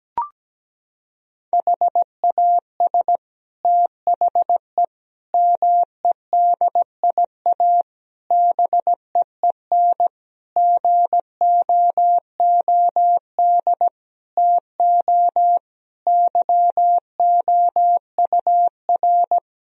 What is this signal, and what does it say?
Morse code, heard as a keyed tone, about 17 wpm, 700 Hz. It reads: HAS THE MEDIA BEEN GOOD TO YOUR
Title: Sentences from Top 1000 Words - Review Part 4 - 17wpm